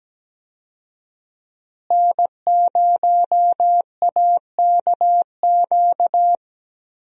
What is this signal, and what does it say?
Morse code, heard as a keyed tone, about 17 wpm, 700 Hz. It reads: N0AKQ
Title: USA Call Signs - 5 Characters - Repeated 3x - 17wpm